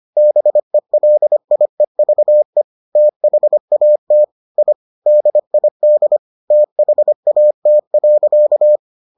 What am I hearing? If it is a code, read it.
Morse code, heard as a keyed tone, about 25 wpm, 600 Hz. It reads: BELIEVE THAT I DID THAT.